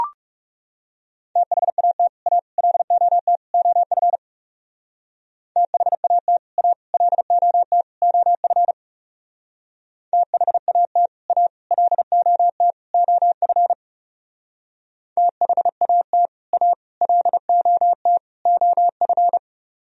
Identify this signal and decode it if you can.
Morse code, heard as a keyed tone, about 37 wpm, 700 Hz. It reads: THAT A LOT OF THAT A LOT OF THAT A LOT OF THAT A LOT OF
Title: Speed-Racing - Sets of 4 Words 30wpm